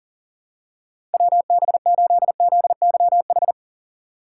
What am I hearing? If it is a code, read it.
Morse code, heard as a keyed tone, about 40 wpm, 700 Hz. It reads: WB8ZYH
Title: USA Call Signs - 6 Characters - 4x Character Spacing - 40wpm